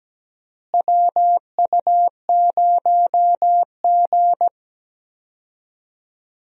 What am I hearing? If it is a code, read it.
Morse code, heard as a keyed tone, about 17 wpm, 700 Hz. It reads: WU0G
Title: USA Call Signs - 4 Characters - Repeated 2x - 17wpm